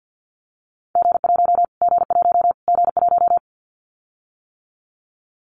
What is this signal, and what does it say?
Morse code, heard as a keyed tone, about 50 wpm, 700 Hz. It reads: G1 G1 G1
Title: Callsign Prefixes for Top 25 DXCC - Repeated 3x - 50wpm